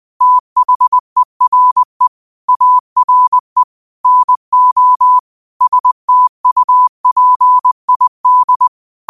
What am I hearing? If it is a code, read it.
Morse code, heard as a keyed tone, about 20 wpm, 1000 Hz. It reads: THERE ARE NO STUPID